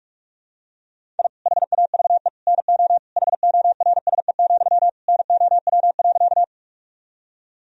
Morse code, heard as a keyed tone, about 47 wpm, 700 Hz. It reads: I HAVE NO HORSE, NOW.
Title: Sentences from Top 1000 Words - Review Part 1 - 45wpm